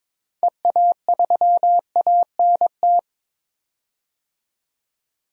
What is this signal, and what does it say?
Morse code, heard as a keyed tone, about 22 wpm, 700 Hz. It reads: EA3ANT
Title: DXCC-100 Call Signs - 6 Characters - 22wpm